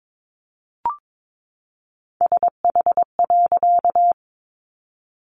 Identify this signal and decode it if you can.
Morse code, heard as a keyed tone, about 22 wpm, 700 Hz. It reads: SH.
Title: A Sound Of Thunder 22wpm